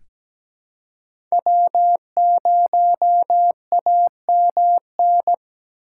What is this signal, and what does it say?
Morse code, heard as a keyed tone, about 17 wpm, 700 Hz. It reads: W0AMN
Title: USA Call Signs - 5 Characters - 3x Character Spacing - 17wpm